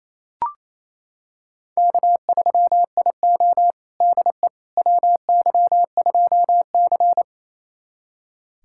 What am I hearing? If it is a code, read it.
Morse code, heard as a keyed tone, about 28 wpm, 700 Hz. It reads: K3IO DE WY2C
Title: USA Callsign DE Callsign - 4 Characters - 28wpm